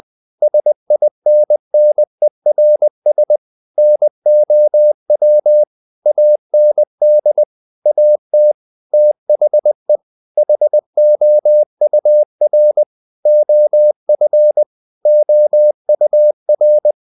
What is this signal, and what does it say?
Morse code, heard as a keyed tone, about 20 wpm, 600 Hz. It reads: SINNERS NOW AND AT THE HOUR OF OUR
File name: the old man and the sea / the old man and the sea part 1 20wpm